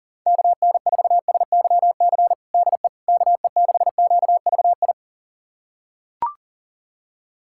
Morse code, heard as a keyed tone, about 40 wpm, 700 Hz. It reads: KN4SYC DE KE6QVI
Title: USA Callsign DE Callsign - 6 Characters - 40wpm